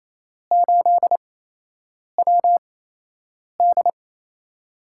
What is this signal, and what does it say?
Morse code, heard as a keyed tone, about 28 wpm, spaced as for 8 wpm, 700 Hz. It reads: 8WD
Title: USA Call Signs - 5 Characters - 8x Character Spacing - 28wpm